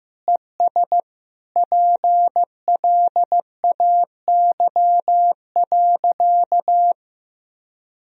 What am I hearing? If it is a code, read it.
Morse code, heard as a keyed tone, about 15 wpm, 700 Hz. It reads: ES PLAY.
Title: Sentences from Top 200 Words and Abbreviations 15wpm